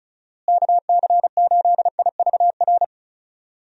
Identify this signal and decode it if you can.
Morse code, heard as a keyed tone, about 36 wpm, 700 Hz. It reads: KC8IVR
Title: USA Call Signs - 6 Characters - Repeated 2x - 35wpm